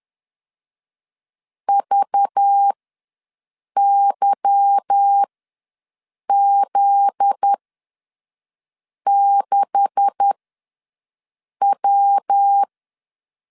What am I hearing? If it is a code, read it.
Morse code, heard as a keyed tone, about 10 wpm, spaced as for 5 wpm, 800 Hz. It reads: VYZ6W
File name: LEZIONI / 9 286 ws 10wpm 4032